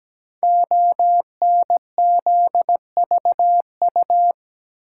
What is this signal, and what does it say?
Morse code, heard as a keyed tone, about 17 wpm, 700 Hz. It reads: ONZVU
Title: Code Groups - 5 Letter:Numbers - 17wpm